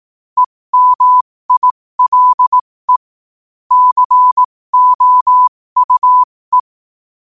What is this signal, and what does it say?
Morse code, heard as a keyed tone, about 18 wpm, spaced as for 15 wpm, 1000 Hz. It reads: EMILE COUE
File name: qotdfiles / 15wpm